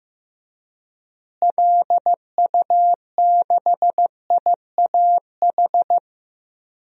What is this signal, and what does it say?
Morse code, heard as a keyed tone, about 15 wpm, 700 Hz. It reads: LU6IAH
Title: DXCC-100 Call Signs - 6 Characters - Repeated 2x - 15wpm